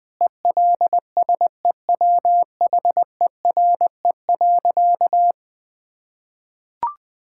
Morse code, heard as a keyed tone, about 20 wpm, 700 Hz. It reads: ELSEWHERE.
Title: Sentences from Top 1600 Words 20wpm